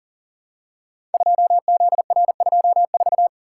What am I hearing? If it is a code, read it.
Morse code, heard as a keyed tone, about 41 wpm, 700 Hz. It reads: 2ZR24
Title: Code Groups - 5 Letter:Numbers - 40wpm